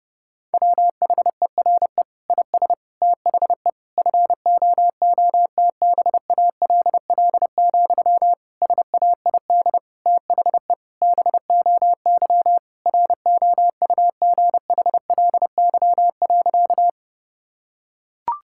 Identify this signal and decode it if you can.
Morse code, heard as a keyed tone, about 30 wpm, 700 Hz. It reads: WHERE IS THE FOOTBALL, SAID THE BOY ROUGHLY.
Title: Sentences from Top 1600 Words 30wpm